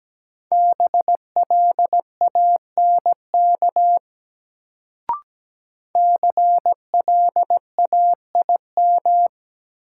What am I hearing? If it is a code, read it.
Morse code, heard as a keyed tone, about 17 wpm, 700 Hz. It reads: BLANK CLAIM
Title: Sentences from Top 1500 Words - New Word Drill - 17wpm